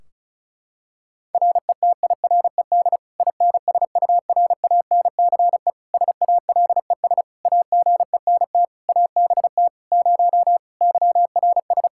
Morse code, heard as a keyed tone, about 36 wpm, 700 Hz. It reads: RETIRED INSURANCE SALES AGENT ABT 0 YRS
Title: Practice QSO - Exchange 3 - More Info - 35wpm